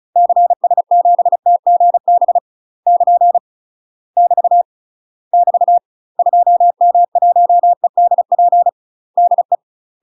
Morse code, heard as a keyed tone, about 36 wpm, 700 Hz. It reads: CS7TGB <KN> <BT> <BT> 2M1EDP DE
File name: UK MASTER QSOs TRAINING / 35WPM / UK MASTER QSOs 0006